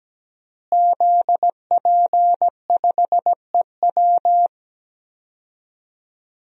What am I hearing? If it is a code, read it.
Morse code, heard as a keyed tone, about 17 wpm, 700 Hz. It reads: ZP5EW
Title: DXCC-100 Call Signs - 5 Characters - Repeated 2x - 17wpm